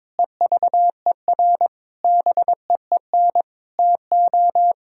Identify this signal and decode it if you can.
Morse code, heard as a keyed tone, about 22 wpm, 700 Hz. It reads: EVER BEEN TO
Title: Sentences from Top 1000 Words - Review Part 4 - 22wpm